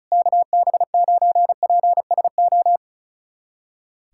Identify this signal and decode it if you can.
Morse code, heard as a keyed tone, about 34 wpm, 700 Hz. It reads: KB9PSO
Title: USA Callsign DE Callsign 35wpm